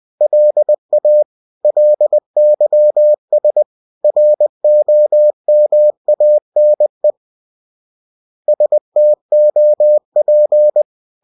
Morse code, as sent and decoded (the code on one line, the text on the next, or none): .-.. .- / .-.. -.-- ... / .-. --- -- .- -. / ... - --- .--.
LA LYS ROMANE STOP